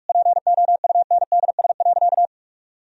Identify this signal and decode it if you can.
Morse code, heard as a keyed tone, about 46 wpm, 700 Hz. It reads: POUNDS.
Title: Sentences from Top 1000 Words - Review Part 4 - 45wpm